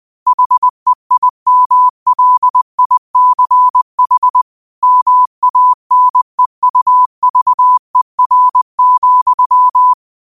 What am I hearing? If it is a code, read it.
Morse code, heard as a keyed tone, about 20 wpm, 1000 Hz. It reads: HEIMLICH MANEUVER,
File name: 20wpm